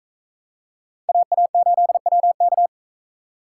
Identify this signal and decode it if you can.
Morse code, heard as a keyed tone, about 38 wpm, 700 Hz. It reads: AA8WK